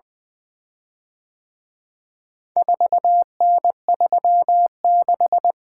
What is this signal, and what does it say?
Morse code, heard as a keyed tone, about 24 wpm, 700 Hz. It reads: 4N36
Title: Speed-Racing - Four Letter-Number Combinations - 20wpm